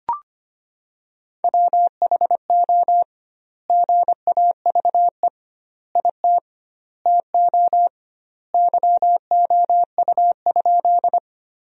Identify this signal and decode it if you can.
Morse code, heard as a keyed tone, about 25 wpm, 700 Hz. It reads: WHO GAVE IT TO YOU?
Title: 2x Word-Spacing - Sentences from Top 500 Words - 25wpm